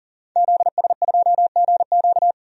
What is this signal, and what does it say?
Morse code, heard as a keyed tone, about 39 wpm, 700 Hz. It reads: ZS2GQ
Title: DXCC-100 Call Signs - 5 Characters - 40wpm